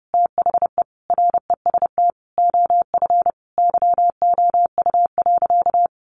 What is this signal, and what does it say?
Morse code, heard as a keyed tone, about 30 wpm, 700 Hz. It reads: THE REST OF YOU.
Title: Sentences from Top 700 Words 30wpm